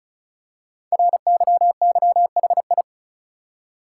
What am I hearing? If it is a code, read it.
Morse code, heard as a keyed tone, about 36 wpm, 690 Hz. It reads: RYYHI
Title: Code Groups - 5 Letters - 35wpm